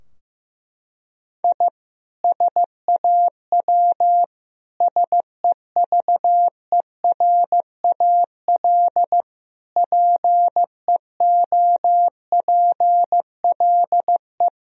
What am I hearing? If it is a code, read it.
Morse code, heard as a keyed tone, about 15 wpm, 700 Hz. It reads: I SAW SEVERAL PEOPLE
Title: Sentences from Top 1000 Words - Review Part 2 - 15wpm